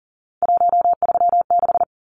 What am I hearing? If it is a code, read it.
Morse code, heard as a keyed tone, about 40 wpm, 700 Hz. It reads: 136
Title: Numbers - 3-Digit 40wpm